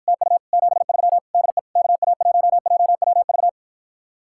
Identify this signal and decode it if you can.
Morse code, heard as a keyed tone, about 53 wpm, 700 Hz. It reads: TU 73 DE KA1JWV